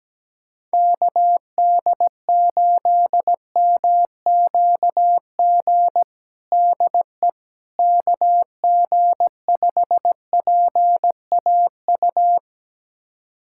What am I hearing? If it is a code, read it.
Morse code, heard as a keyed tone, about 17 wpm, 700 Hz. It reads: KD8MQG DE KG5PAU
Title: USA Callsign DE Callsign 17wpm